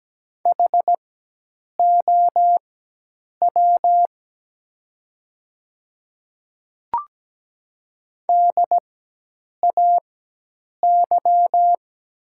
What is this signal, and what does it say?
Morse code, heard as a keyed tone, about 17 wpm, spaced as for 8 wpm, 700 Hz. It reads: HOW DAY
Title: 4x Character Spacing - Top 100 Words - 17wpm